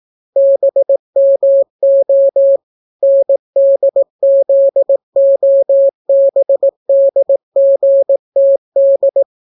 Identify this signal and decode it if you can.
Morse code, heard as a keyed tone, about 18 wpm, 550 Hz. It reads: BMO NDZOBDGTD